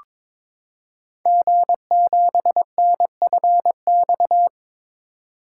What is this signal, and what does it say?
Morse code, heard as a keyed tone, about 22 wpm, 690 Hz. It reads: G7NFX
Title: Code Groups - 5 Letter:Numbers - 22wpm